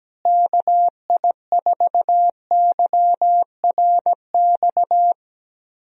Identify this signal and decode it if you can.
Morse code, heard as a keyed tone, about 17 wpm, 700 Hz. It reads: KI4YRX
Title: USA Call Signs - 6 Characters - Repeated 2x - 17wpm